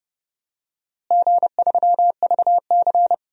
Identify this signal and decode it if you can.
Morse code, heard as a keyed tone, about 32 wpm, 700 Hz. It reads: G3VC
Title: DXCC-100 Call Signs - 4 Characters - Repeated 3x - 30wpm